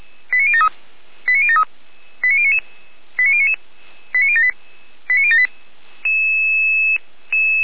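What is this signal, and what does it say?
Five-tone selective call (ZVEI), heard as quick runs of five tones, tones of about 70 ms. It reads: 89073 89073 89099 89078 89078
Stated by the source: ZVEI five-tone call received by SDR.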